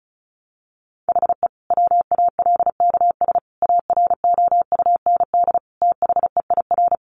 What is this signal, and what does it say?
Morse code, heard as a keyed tone, about 34 wpm, 700 Hz. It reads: HE WALKS AROUND THEIR
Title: Easy Phrases 35wpm